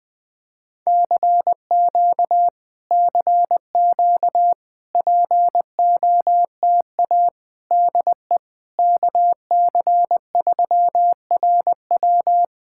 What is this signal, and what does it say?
Morse code, heard as a keyed tone, about 20 wpm, 700 Hz. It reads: CQ CQ POTA DE KC3RW
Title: POTA QSO Practice - 20wpm